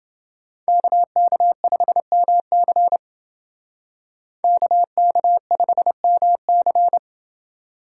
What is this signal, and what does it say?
Morse code, heard as a keyed tone, about 28 wpm, 700 Hz. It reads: KK5MC KK5MC